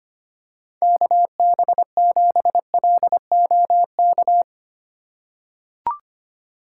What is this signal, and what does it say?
Morse code, heard as a keyed tone, about 25 wpm, 700 Hz. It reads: KB7LOK
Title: USA Call Signs - 6 Characters - 4x Character Spacing - 25wpm